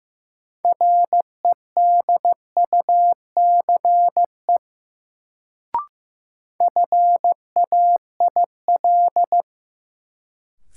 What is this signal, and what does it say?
Morse code, heard as a keyed tone, about 15 wpm, 700 Hz. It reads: REDUCE FAIL
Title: Sentences from Top 1100 Words - New Word Drill - 15wpm